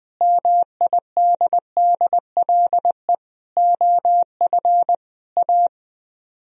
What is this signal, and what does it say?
Morse code, heard as a keyed tone, about 20 wpm, 700 Hz. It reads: MIDDLE OF A
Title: Speed-Racing - Sets of 5 Words 17wpm